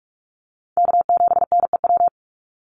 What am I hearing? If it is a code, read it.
Morse code, heard as a keyed tone, about 44 wpm, 690 Hz. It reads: K7NEW
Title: NA SOTA Chaser Callsigns 45wpm